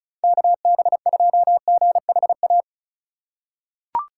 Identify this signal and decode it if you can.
Morse code, heard as a keyed tone, about 36 wpm, 700 Hz. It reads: KB2GHA
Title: USA Call Signs - 6 Characters - 4x Character Spacing - 35wpm